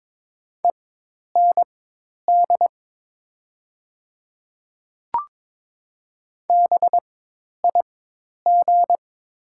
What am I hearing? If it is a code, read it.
Morse code, heard as a keyed tone, about 22 wpm, spaced as for 10 wpm, 700 Hz. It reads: END BIG
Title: 4x Character Spacing - Top 300 Words - 22wpm